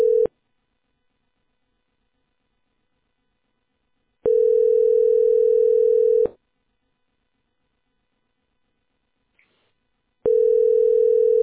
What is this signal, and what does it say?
Ringback tone (North America), 440 + 480 Hz.